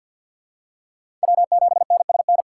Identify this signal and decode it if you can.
Morse code, heard as a keyed tone, about 53 wpm, 700 Hz. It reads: W7NSN
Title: USA Call Signs - 4x Character Spacing - 50wpm